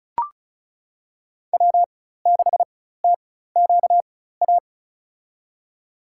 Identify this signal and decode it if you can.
Morse code, heard as a keyed tone, about 36 wpm, spaced as for 16 wpm, 700 Hz. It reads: W6TQA
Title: USA Call Signs - 4x Character Spacing - 35wpm